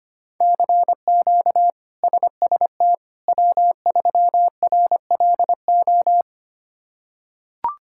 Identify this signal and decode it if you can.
Morse code, heard as a keyed tone, about 25 wpm, 700 Hz. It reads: CQ SST W3RLO
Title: K1USN SST QSO Practice - 25wpm